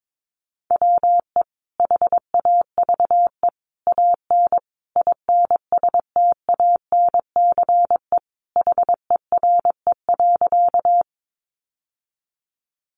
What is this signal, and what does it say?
Morse code, heard as a keyed tone, about 22 wpm, 700 Hz. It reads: WE HAVE AN INSTANCE HERE.
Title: Sentences from Top 800 Words 22wpm